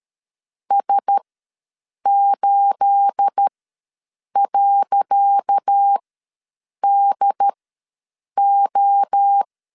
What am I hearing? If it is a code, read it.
Morse code, heard as a keyed tone, about 13 wpm, spaced as for 7 wpm, 800 Hz. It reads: S8.DO
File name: LEZIONI / 10 SIGNS / 10C DOT ws 12wpm 3548